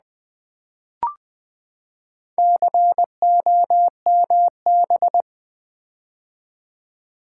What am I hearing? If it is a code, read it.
Morse code, heard as a keyed tone, about 20 wpm, 700 Hz. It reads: COMB
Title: Build A Word - 4 Letters At A Time - 11-Letter Words - 20wpm